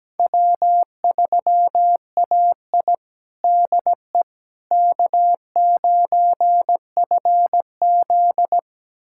Morse code, heard as a keyed tone, about 17 wpm, 700 Hz. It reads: W3AI DE K9FZ
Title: USA Callsign DE Callsign - 4 Characters - 17wpm